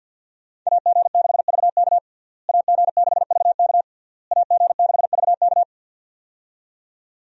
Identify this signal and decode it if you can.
Morse code, heard as a keyed tone, about 52 wpm, 700 Hz. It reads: AG6VK AG6VK AG6VK
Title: USA Call Signs - 5 Characters - Repeated 3x - 50wpm